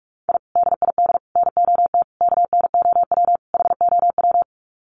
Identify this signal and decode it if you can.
Morse code, heard as a keyed tone, about 44 wpm, 700 Hz. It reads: I DID NOT KNOW HOW
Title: Sets of 5 Words Encore 45wpm